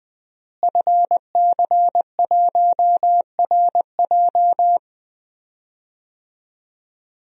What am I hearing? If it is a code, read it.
Morse code, heard as a keyed tone, about 20 wpm, 700 Hz. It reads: FC1RJ
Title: DXCC-100 Call Signs - 5 Characters - Repeated 2x - 20wpm